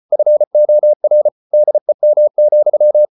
Morse code, heard as a keyed tone, about 34 wpm, 600 Hz. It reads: FOR DEM,
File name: Ferie QRQ 34wpm